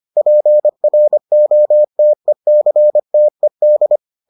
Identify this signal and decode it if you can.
Morse code, heard as a keyed tone, about 25 wpm, 600 Hz. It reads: PROTECTED